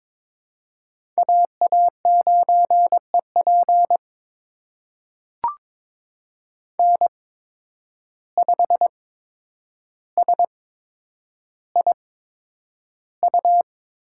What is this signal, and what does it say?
Morse code, heard as a keyed tone, about 22 wpm, spaced as for 9 wpm, 700 Hz. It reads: AA9EP N5SIU
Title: USA Call Signs - 5 Characters - 8x Character Spacing - 22wpm